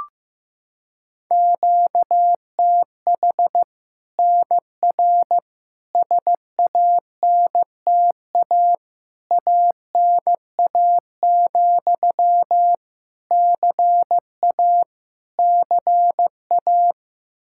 Morse code, heard as a keyed tone, about 15 wpm, 700 Hz. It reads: QTH NR SANTA ANA, CA CA
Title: Practice QSO - Exchange 1 - Beginning - 15wpm